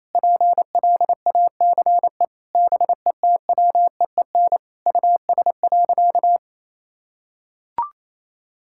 Morse code, heard as a keyed tone, about 28 wpm, 700 Hz. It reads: PLACE BETWEEN US.
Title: Sentences from Top 1000 Words - Review Part 6 - 28wpm